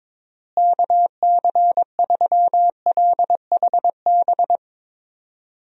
Morse code, heard as a keyed tone, about 22 wpm, 700 Hz. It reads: KC3LHB
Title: USA Call Signs - 6 Characters - Repeated 3x - 22wpm